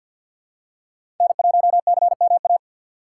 Morse code, heard as a keyed tone, about 54 wpm, 700 Hz. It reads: N1CMA
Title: USA Call Signs - 6x Character Spacing - 50wpm